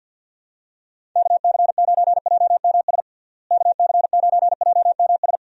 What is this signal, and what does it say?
Morse code, heard as a keyed tone, about 49 wpm, 700 Hz. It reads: KC9JMS KC9JMS
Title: USA Call Signs - 6 Characters - Repeated 2x - 50wpm